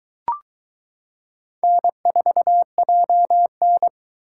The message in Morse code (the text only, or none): N4JN